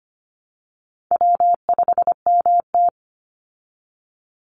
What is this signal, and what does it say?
Morse code, heard as a keyed tone, about 25 wpm, 700 Hz. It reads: W5MT
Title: Easy Call Signs Repeated 3x - 25wpm